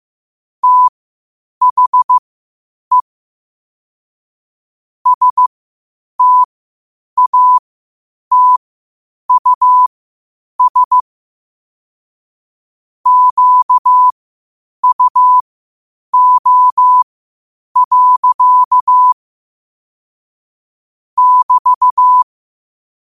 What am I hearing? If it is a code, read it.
Morse code, heard as a keyed tone, about 15 wpm, spaced as for 8 wpm, 1000 Hz. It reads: THE STATUS QUO. <BT>
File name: qotdfiles / 8wpm